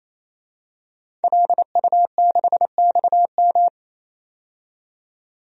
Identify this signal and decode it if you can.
Morse code, heard as a keyed tone, about 28 wpm, 700 Hz. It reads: LU6XM
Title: DXCC-100 Call Signs - 5 Characters - 28wpm